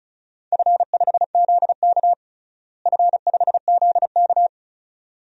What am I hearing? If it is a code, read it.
Morse code, heard as a keyed tone, about 36 wpm, 700 Hz. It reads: F5ZK F5ZK
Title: DXCC-100 Call Signs - 4 Characters - Repeated 2x - 35wpm